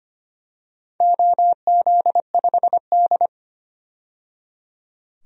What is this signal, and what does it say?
Morse code, heard as a keyed tone, about 25 wpm, 700 Hz. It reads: OZ5D
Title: DXCC-100 Call Signs - 4 Characters - 25wpm